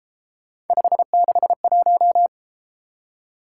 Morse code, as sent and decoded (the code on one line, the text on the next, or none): ..... -.... .----
561